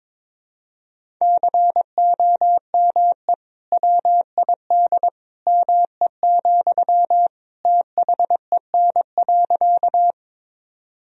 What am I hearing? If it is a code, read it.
Morse code, heard as a keyed tone, about 21 wpm, 700 Hz. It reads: COME WID ME, THEN.
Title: Sentences from Top 200 Words and Abbreviations 22wpm